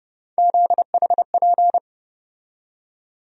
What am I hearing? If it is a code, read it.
Morse code, heard as a keyed tone, about 31 wpm, 700 Hz. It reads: ZHP
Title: USA Call Sign Postfixes - 3 Letters 30wpm